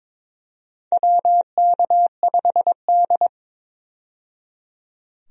Morse code, heard as a keyed tone, about 21 wpm, 700 Hz. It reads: WK5D
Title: USA Call Signs - 4 Characters - 22wpm